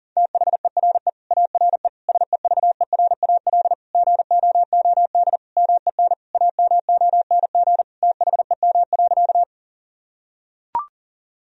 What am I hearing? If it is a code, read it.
Morse code, heard as a keyed tone, about 40 wpm, 700 Hz. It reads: THERE ARE SEVERAL GOOD MEN AMONG THEM.
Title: Sentences from Top 1000 Words - Review Part 2 - 40wpm